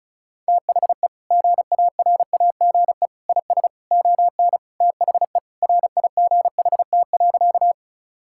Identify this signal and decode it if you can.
Morse code, heard as a keyed tone, about 34 wpm, 700 Hz. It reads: THE GARAGE IS ON THE RIGHT.
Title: Sentences from Top 1600 Words 35wpm